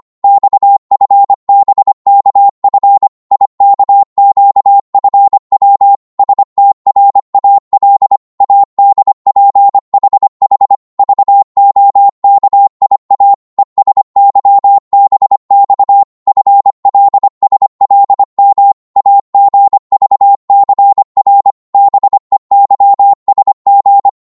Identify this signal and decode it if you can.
Morse code, heard as a keyed tone, about 25 wpm, 800 Hz. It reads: XFBKF IKQFW STRAL ADPHH VOKIA ESYBX FLSLM AGVCR BEYSG
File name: Morse / ltr-25wpm